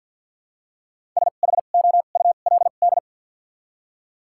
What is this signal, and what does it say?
Morse code, heard as a keyed tone, about 53 wpm, spaced as for 37 wpm, 700 Hz. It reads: SHOULD
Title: 2x Character Spacing - Top 300 Words - 50wpm